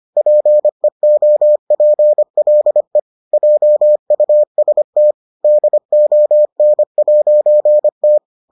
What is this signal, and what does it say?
Morse code, heard as a keyed tone, about 25 wpm, 600 Hz. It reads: PEOPLE JUST DON'T